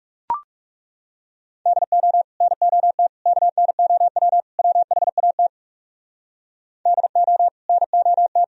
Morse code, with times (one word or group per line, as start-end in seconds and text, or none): 1.7-2.2 DO
2.4-3.1 NOT
3.3-4.4 KNOW
4.6-5.5 WHAT
6.9-7.5 DO
7.7-8.4 NOT